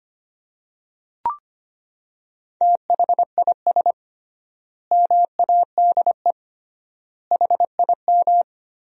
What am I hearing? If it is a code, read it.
Morse code, heard as a keyed tone, about 25 wpm, 700 Hz. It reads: THIS MADE HIM
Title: 3x Word-Spacing - Sentences from Top 500 Words - 25wpm